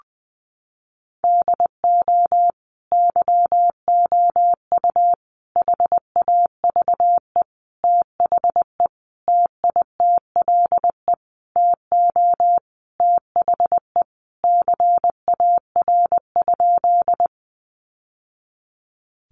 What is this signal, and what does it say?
Morse code, heard as a keyed tone, about 20 wpm, 700 Hz. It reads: DO YOU HAVE THE TITLE TO THE CAR?
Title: Sentences from Top 1200 Words 20wpm